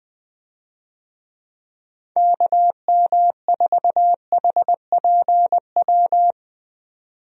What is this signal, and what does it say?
Morse code, heard as a keyed tone, about 20 wpm, 700 Hz. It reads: KM4HPW
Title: USA Call Signs - 6 Characters - 20wpm